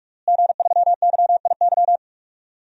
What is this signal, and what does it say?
Morse code, heard as a keyed tone, about 45 wpm, 700 Hz. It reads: G3YIY
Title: DXCC-100 Call Signs - 5 Characters - Repeated 2x - 45wpm